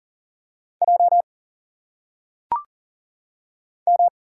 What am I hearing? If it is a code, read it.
Morse code, heard as a keyed tone, about 40 wpm, 700 Hz. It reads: J M
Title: Single Letters - Phonetic Pronunciation 40wpm